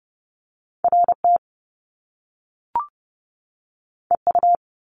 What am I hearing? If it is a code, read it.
Morse code, heard as a keyed tone, about 30 wpm, 700 Hz. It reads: RT EU
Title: Top Two Letter Combinations in Words 30wpm